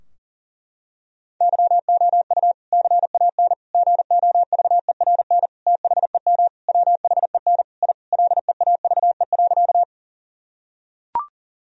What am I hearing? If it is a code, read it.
Morse code, heard as a keyed tone, about 40 wpm, 700 Hz. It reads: YOU CAN GOVERN THEM WHEN I LEAVE.
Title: Sentences from Top 1000 Words - Review Part 2 - 40wpm